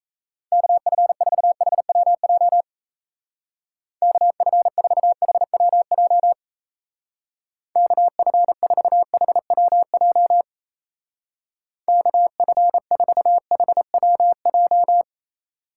Morse code, heard as a keyed tone, about 34 wpm, 700 Hz. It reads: KF4HWJ KF4HWJ KF4HWJ KF4HWJ
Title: Speed Racing - USA 2x3 Call Signs 28wpm